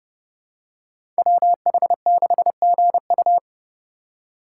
Morse code, heard as a keyed tone, about 30 wpm, 700 Hz. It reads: WH6GU
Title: USA Call Signs - 5 Characters - 3x Character Spacing - 30wpm